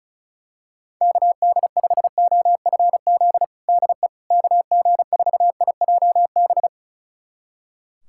Morse code, heard as a keyed tone, about 36 wpm, 700 Hz. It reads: KD5OFZ DE KG4IJB
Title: USA Callsign DE Callsign - 6 Characters - 35wpm